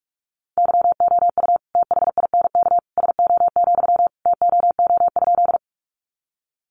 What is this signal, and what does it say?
Morse code, heard as a keyed tone, about 44 wpm, 700 Hz. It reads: YOU THINK SO, TOO?
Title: Sentences from Top 200 Words 45wpm